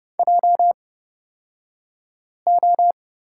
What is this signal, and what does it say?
Morse code, heard as a keyed tone, about 29 wpm, spaced as for 5 wpm, 700 Hz. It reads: JO
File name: Lesson 36d - BT - Two Characters 30wpm